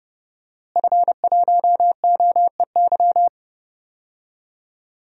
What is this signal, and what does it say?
Morse code, heard as a keyed tone, about 31 wpm, 700 Hz. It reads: F1OEY
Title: DXCC-100 Call Signs - 5 Characters - Repeated 3x - 30wpm